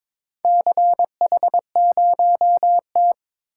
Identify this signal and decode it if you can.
Morse code, heard as a keyed tone, about 22 wpm, 700 Hz. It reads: CH0T
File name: practice / 22wpm / Four Letter-Number Combinations - 22wpm